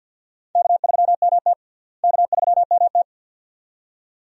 Morse code, heard as a keyed tone, about 46 wpm, 700 Hz. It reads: K3MT K3MT
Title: USA Call Signs - 4 Characters - Repeated 2x - 50wpm